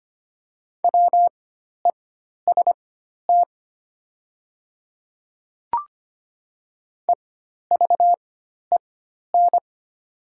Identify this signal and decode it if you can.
Morse code, heard as a keyed tone, about 25 wpm, spaced as for 12 wpm, 690 Hz. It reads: WEST EVEN
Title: 4x Character Spacing - Top 500 Words - 25wpm